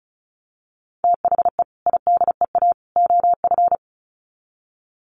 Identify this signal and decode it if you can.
Morse code, heard as a keyed tone, about 34 wpm, 700 Hz. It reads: THE IDEA OF
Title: Sets of 3 Words 35wpm